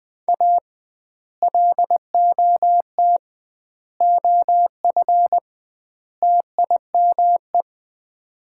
Morse code, heard as a keyed tone, about 20 wpm, 700 Hz. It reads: A LOT OF TIME
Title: 2x Word-Spacing - Sets of 4 Words - 20wpm